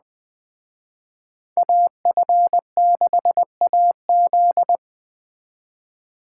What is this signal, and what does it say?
Morse code, heard as a keyed tone, about 20 wpm, 700 Hz. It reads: AF6AZ
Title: Speed Racing - USA 2x2 Call Signs 17wpm